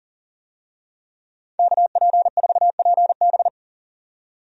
Morse code, heard as a keyed tone, about 40 wpm, 700 Hz. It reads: KP4PB
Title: DXCC-100 Call Signs - 5 Characters - 40wpm